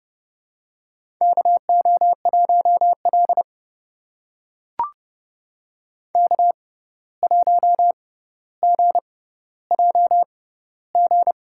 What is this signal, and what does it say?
Morse code, heard as a keyed tone, about 30 wpm, spaced as for 14 wpm, 700 Hz. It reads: KO1L K1GJG
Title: USA Call Signs - 6x Character Spacing - 30wpm